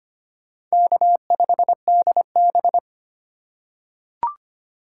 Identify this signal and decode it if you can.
Morse code, heard as a keyed tone, about 25 wpm, 700 Hz. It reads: K5DB
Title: USA Call Signs - 4 Characters - 2x Character Spacing - 25wpm